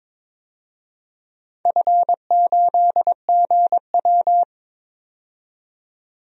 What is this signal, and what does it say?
Morse code, heard as a keyed tone, about 22 wpm, 700 Hz. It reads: F8GW
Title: DXCC-100 Call Signs - 4 Characters - 22wpm